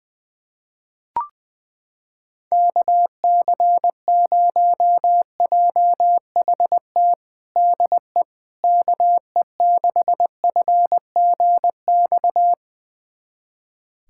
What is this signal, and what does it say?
Morse code, heard as a keyed tone, about 20 wpm, 700 Hz. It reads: KC0JHT DE KE6FGX
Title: USA Callsign DE Callsign - 6 Characters - 20wpm